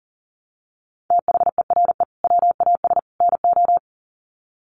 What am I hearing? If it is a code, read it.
Morse code, heard as a keyed tone, about 40 wpm, 700 Hz. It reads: THERE WAS NO
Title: Sets of 3 Words 40wpm